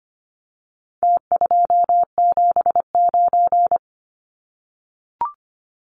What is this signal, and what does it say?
Morse code, heard as a keyed tone, about 25 wpm, 700 Hz. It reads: T279
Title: Numbers - Contest Style - 4-Digit 25wpm